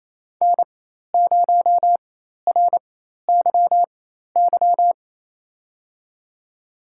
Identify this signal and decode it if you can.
Morse code, heard as a keyed tone, about 29 wpm, spaced as for 13 wpm, 700 Hz. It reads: N0RYY
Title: USA Call Signs - 5 Characters - 4x Character Spacing - 28wpm